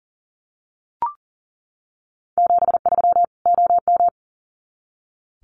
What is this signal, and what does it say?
Morse code, heard as a keyed tone, about 39 wpm, 700 Hz. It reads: 73 OM
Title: Common QSO Phrases 40wpm